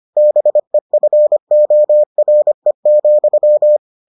Morse code, heard as a keyed tone, about 25 wpm, 600 Hz. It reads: BEFORE,